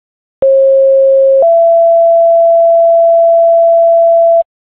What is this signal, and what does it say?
Two-tone page, heard as a one-second tone, then a three-second tone, about 540 then 670 Hz. It is heard from 0.4 to 4.4 s.